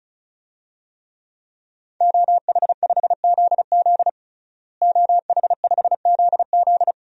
Morse code, heard as a keyed tone, about 34 wpm, 700 Hz. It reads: OH5ZZ OH5ZZ
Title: DXCC-100 Call Signs - 5 Characters - Repeated 3x - 35wpm